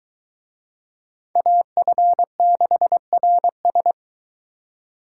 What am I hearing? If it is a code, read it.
Morse code, heard as a keyed tone, about 22 wpm, 700 Hz. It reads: AF6RS